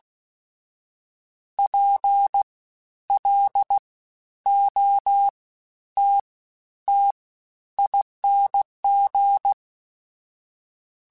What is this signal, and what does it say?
Morse code, heard as a keyed tone, about 16 wpm, 800 Hz. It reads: P L O T T ING
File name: Special INGs-mp3s / Special INGs-mp3s / Fast ing @15